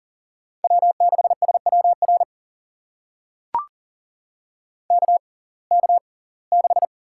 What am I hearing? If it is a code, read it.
Morse code, heard as a keyed tone, about 40 wpm, spaced as for 24 wpm, 700 Hz. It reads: W6SWR KK6
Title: USA Call Signs - 5 Characters - 6x Character Spacing - 40wpm